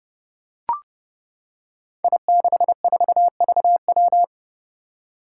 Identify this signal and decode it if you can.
Morse code, heard as a keyed tone, about 32 wpm, 700 Hz. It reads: I64VW
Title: Five Challenging Letter-Number Combinations 30wpm